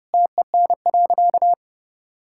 Morse code, heard as a keyed tone, about 32 wpm, 700 Hz. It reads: TEN.
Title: Sentences from Top 1000 Words - Review Part 5 - 30wpm